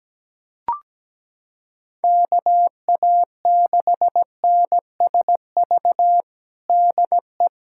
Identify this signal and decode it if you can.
Morse code, heard as a keyed tone, about 17 wpm, 700 Hz. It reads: KA6NSV DE
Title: USA Callsign DE Callsign 17wpm